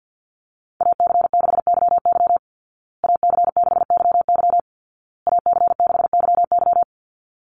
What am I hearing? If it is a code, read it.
Morse code, heard as a keyed tone, about 50 wpm, 700 Hz. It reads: AC6YY AC6YY AC6YY
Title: Call Signs Repeated 2x - 50wpm